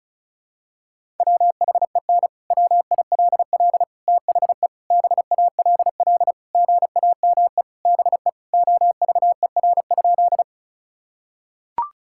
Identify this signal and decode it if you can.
Morse code, heard as a keyed tone, about 36 wpm, 700 Hz. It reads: WHEN WILL THE BALL GAME BE OVER?
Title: Sentences from Top 1000 Words - Review Part 6 - 35wpm